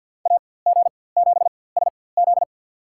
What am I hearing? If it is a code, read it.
Morse code, heard as a keyed tone, about 46 wpm, spaced as for 23 wpm, 700 Hz. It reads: AG7SZ
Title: USA Call Signs - 4x Character Spacing - 50wpm